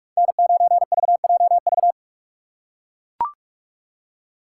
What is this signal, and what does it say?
Morse code, heard as a keyed tone, about 44 wpm, 700 Hz. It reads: N9VJV
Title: USA Call Signs - 5 Characters - 2x Character Spacing - 45wpm